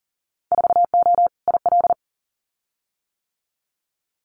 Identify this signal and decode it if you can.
Morse code, heard as a keyed tone, about 42 wpm, 700 Hz. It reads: 4O IL
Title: Winter Field Day Exchanges 40wpm